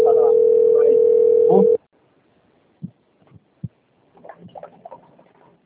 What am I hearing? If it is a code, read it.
Ringback tone (North America), 440 + 480 Hz, with speech.
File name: rg-1600-unknown- - - .56844